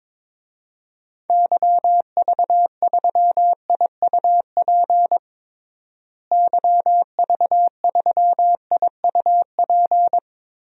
Morse code, heard as a keyed tone, about 22 wpm, 700 Hz. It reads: YV3IUP YV3IUP
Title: DXCC-100 Call Signs - 6 Characters - Repeated 3x - 22wpm